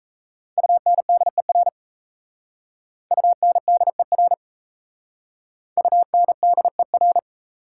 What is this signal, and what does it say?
Morse code, heard as a keyed tone, about 36 wpm, 690 Hz. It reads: UNDER UNDER UNDER